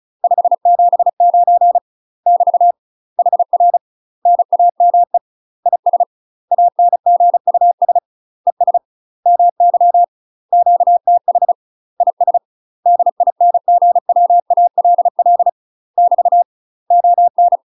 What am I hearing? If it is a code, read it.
Morse code, heard as a keyed tone, about 34 wpm, 700 Hz. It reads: 579 <BT> HR NAME IS ANGUS ES MY QTH IS DINGWALL <BT> ON
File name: UK MASTER QSOs TRAINING / 35WPM / UK MASTER QSOs 0005